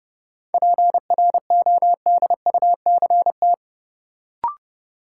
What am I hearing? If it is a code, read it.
Morse code, heard as a keyed tone, about 32 wpm, 700 Hz. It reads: PRODUCT
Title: Sentences from Top 1000 Words - Review Part 2 - New Word Drill - 30wpm